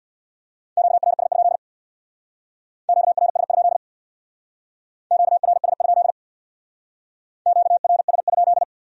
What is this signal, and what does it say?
Morse code, heard as a keyed tone, about 59 wpm, 700 Hz. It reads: QRS? QRS? QRS? QRS?
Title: Speed-Racing Q-Codes - 50wpm